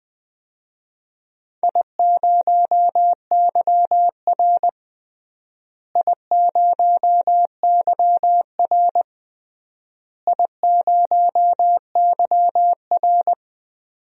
Morse code, heard as a keyed tone, about 20 wpm, 700 Hz. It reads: I0YR I0YR I0YR
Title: DXCC-100 Call Signs - 4 Characters - Repeated 3x - 20wpm